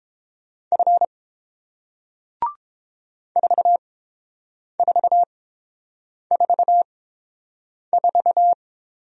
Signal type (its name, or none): Morse code